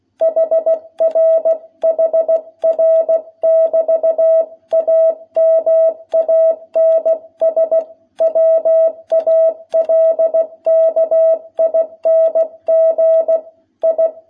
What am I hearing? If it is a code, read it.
Morse code, heard as a keyed tone, about 16 wpm, 660 Hz. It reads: HRHR <BT> A MANS WALKING I
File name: jh8cbh / 2021cw